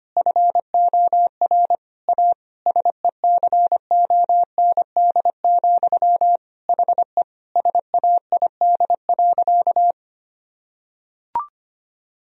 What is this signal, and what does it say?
Morse code, heard as a keyed tone, about 25 wpm, 700 Hz. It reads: FOR A SECOND, HE SAID.
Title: Sentences from Top 1000 Words - Review Part 1 - 25wpm